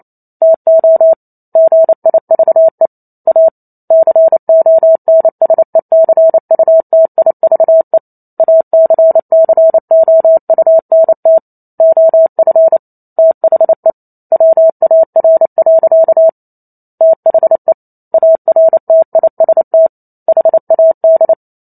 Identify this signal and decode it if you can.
Morse code, heard as a keyed tone, about 28 wpm, 650 Hz. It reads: TO GIVE A CONSECUTIVE ACCOUNT OF THE WAR. THE ARTIST HAD